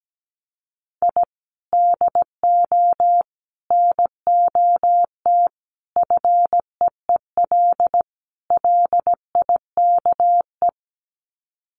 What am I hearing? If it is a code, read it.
Morse code, heard as a keyed tone, about 17 wpm, 700 Hz. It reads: I DO NOT FEEL LIKE
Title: Sets of 5 Words Encore 17wpm